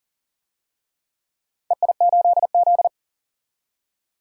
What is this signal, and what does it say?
Morse code, heard as a keyed tone, about 40 wpm, 700 Hz. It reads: EI8Z
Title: DXCC-100 Call Signs - 4 Characters - 40wpm